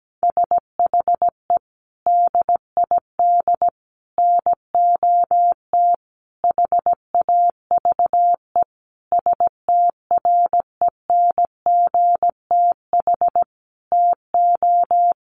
Morse code, heard as a keyed tone, about 17 wpm, 700 Hz. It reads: SHE DID NOT HAVE STRENGTH TO
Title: Sentences from Top 1300 Words 17wpm